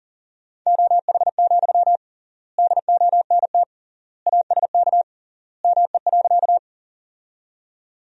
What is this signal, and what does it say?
Morse code, heard as a keyed tone, about 37 wpm, 700 Hz. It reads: OH, DONT ASK ME.
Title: 3x Word-Spacing - Sentences from Top 500 Words - 40wpm